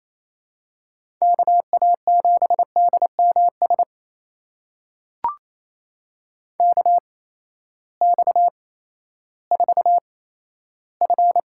Morse code, heard as a keyed tone, about 28 wpm, spaced as for 14 wpm, 700 Hz. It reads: KA7DMS KX4F
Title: USA Call Signs - 8x Character Spacing - 28wpm